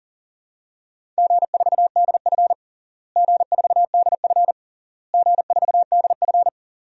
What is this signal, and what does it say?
Morse code, heard as a keyed tone, about 40 wpm, 700 Hz. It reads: G4DF G4DF G4DF
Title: DXCC-100 Call Signs - 4 Characters - Repeated 3x - 40wpm